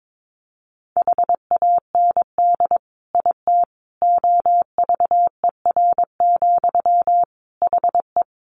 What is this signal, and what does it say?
Morse code, heard as a keyed tone, about 22 wpm, 700 Hz. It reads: HAND IT OVER, HE
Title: Sentences from Top 1500 Words 22wpm